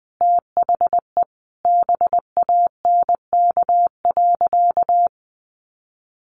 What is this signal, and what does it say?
Morse code, heard as a keyed tone, about 20 wpm, 700 Hz. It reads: THE BANK.
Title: Sentences from Top 1300 Words 20wpm